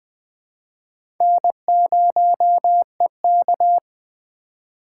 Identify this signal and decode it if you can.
Morse code, heard as a keyed tone, about 20 wpm, 700 Hz. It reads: N0EK
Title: USA Call Signs - 4 Characters - 4x Character Spacing - 20wpm